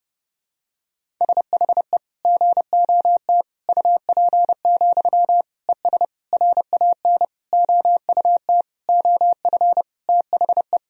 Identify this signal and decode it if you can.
Morse code, heard as a keyed tone, about 30 wpm, 700 Hz. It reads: SHE GOT UP, ES RAN OUT OF THE
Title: Sentences from Top 500 Words and Abbreviations 30wpm